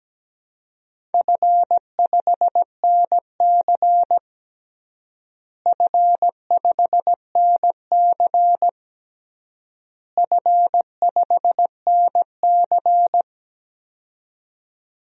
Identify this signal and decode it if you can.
Morse code, heard as a keyed tone, about 17 wpm, 700 Hz. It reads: F5NC F5NC F5NC
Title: DXCC-100 Call Signs - 4 Characters - Repeated 3x - 17wpm